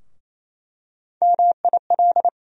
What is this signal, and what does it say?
Morse code, heard as a keyed tone, about 28 wpm, 700 Hz. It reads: MIL
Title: USA Call Sign Postfixes - 3 Letters 28wpm